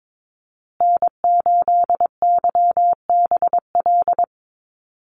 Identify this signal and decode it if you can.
Morse code, heard as a keyed tone, about 22 wpm, 690 Hz. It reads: N8YBL
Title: USA Call Signs Repeated 3x 22wpm